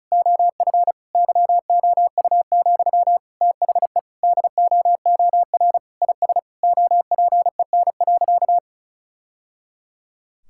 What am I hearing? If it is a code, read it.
Morse code, heard as a keyed tone, about 36 wpm, 700 Hz. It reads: OF YOU, THE DOOR IS OPEN.
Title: Sentences from Top 1000 Words - Review Part 6 - 35wpm